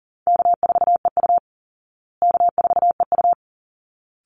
Morse code, heard as a keyed tone, about 40 wpm, 700 Hz. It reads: K4EU K4EU
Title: Call Signs Repeated 3x - 40wpm